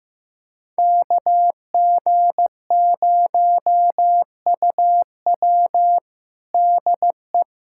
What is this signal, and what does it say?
Morse code, heard as a keyed tone, about 16 wpm, 700 Hz. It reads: KG0UW DE
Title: USA Callsign DE Callsign - 5 Characters - 15wpm